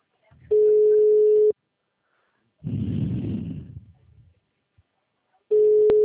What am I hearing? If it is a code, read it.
Ringback tone (Europe, 425 Hz), with speech.
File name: out- -2016- - - .75055